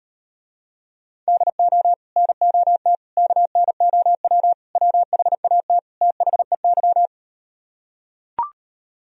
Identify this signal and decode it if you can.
Morse code, heard as a keyed tone, about 38 wpm, 700 Hz. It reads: DO NOT KNOW WHAT THEY